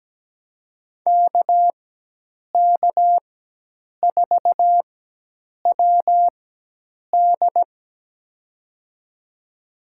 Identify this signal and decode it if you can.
Morse code, heard as a keyed tone, about 17 wpm, spaced as for 8 wpm, 700 Hz. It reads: KK4WD
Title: USA Call Signs - 5 Characters - 4x Character Spacing - 17wpm